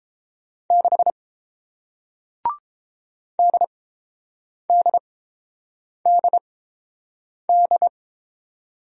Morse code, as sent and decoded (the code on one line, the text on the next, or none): -.... / -.. -.. -.. -..
6 DDDD